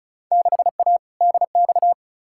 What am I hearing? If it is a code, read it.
Morse code, heard as a keyed tone, about 36 wpm, 700 Hz. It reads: 6A DX
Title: Field Day Exchanges Encore - 35wpm